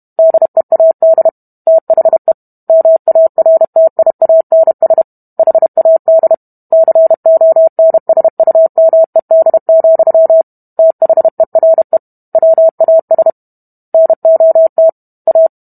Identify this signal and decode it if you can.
Morse code, heard as a keyed tone, about 32 wpm, 650 Hz. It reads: DEAD THE MARTIANS HAD CONSUMED, THERE WAS NOT A